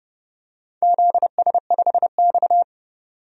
Morse code, heard as a keyed tone, about 30 wpm, 700 Hz. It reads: ZS5X
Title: DXCC-100 Call Signs - 4 Characters - 30wpm